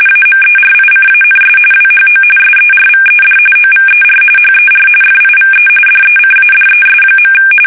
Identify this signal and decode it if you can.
RTTY, about 45.45 baud. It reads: AMERICANS WHO SERVED OVER HER FIVE DECADES.
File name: fsk850 / battleshipnj